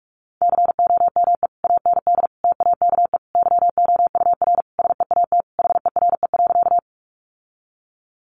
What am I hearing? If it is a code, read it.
Morse code, heard as a keyed tone, about 44 wpm, 700 Hz. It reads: COME AND TAKE YOUR SEAT HERE.
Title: Sentences from Top 1500 Words 45wpm